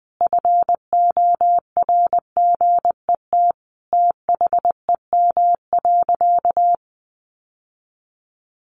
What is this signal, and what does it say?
Morse code, heard as a keyed tone, about 19 wpm, 700 Hz. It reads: FORGET THEM.
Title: Sentences from Top 1200 Words 20wpm